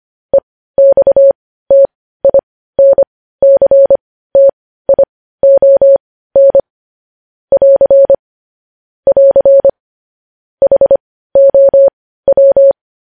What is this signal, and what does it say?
Morse code, heard as a keyed tone, about 25 wpm, spaced as for 15 wpm, 550 Hz. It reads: EXTINCTION <AR> <AR> HOW
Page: News Headlines in Morse Code at 15 WPM